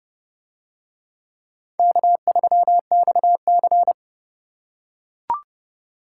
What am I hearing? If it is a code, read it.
Morse code, heard as a keyed tone, about 29 wpm, 700 Hz. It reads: K3XC